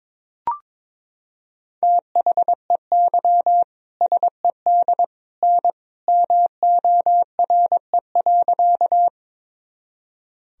Morse code, heard as a keyed tone, about 22 wpm, 700 Hz. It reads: THEY SED N MORE.
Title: Sentences from Top 100 Words and Abbreviations 22wpm